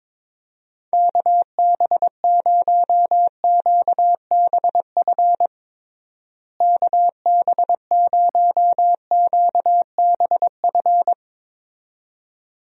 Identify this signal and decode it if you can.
Morse code, heard as a keyed tone, about 22 wpm, 700 Hz. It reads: KB0QBF KB0QBF
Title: USA Call Signs - 6 Characters - Repeated 2x - 22wpm